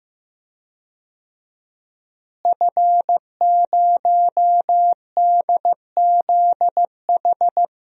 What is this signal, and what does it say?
Morse code, heard as a keyed tone, about 15 wpm, 700 Hz. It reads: F0DZH
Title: DXCC-100 Call Signs - 5 Characters - Repeated 2x - 15wpm